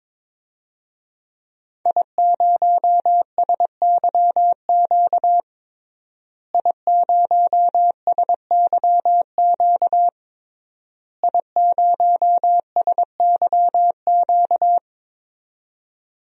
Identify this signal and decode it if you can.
Morse code, heard as a keyed tone, about 21 wpm, 700 Hz. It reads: I0SYQ I0SYQ I0SYQ
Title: DXCC-100 Call Signs - 5 Characters - Repeated 3x - 22wpm